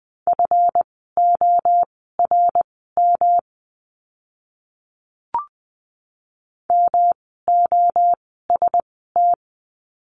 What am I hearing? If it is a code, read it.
Morse code, heard as a keyed tone, about 20 wpm, spaced as for 14 wpm, 700 Hz. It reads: FORM MOST
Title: 139 Lesson 23f - Letter f - Words 20wpm